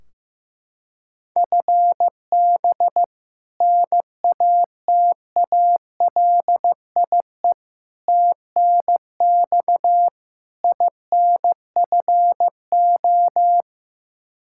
Morse code, heard as a keyed tone, about 16 wpm, 700 Hz. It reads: FB NATALIE TNX INFO
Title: Practice QSO - Exchange 3 - More Info - 15wpm